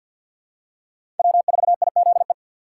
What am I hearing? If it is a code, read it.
Morse code, heard as a keyed tone, about 46 wpm, 690 Hz. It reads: W4IZE
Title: USA Call Signs - 3x Character Spacing - 50wpm